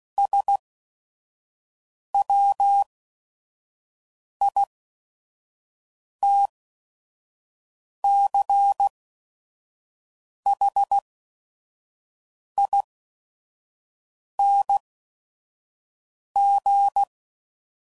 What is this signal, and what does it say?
Morse code, heard as a keyed tone, about 16 wpm, spaced as for 5 wpm, 800 Hz. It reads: SWITCHING
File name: CW Code Groups @5WPM